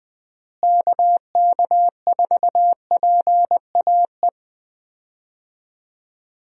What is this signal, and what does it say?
Morse code, heard as a keyed tone, about 21 wpm, 700 Hz. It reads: KK4PAE
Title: USA Call Signs - 6 Characters - Repeated 3x - 20wpm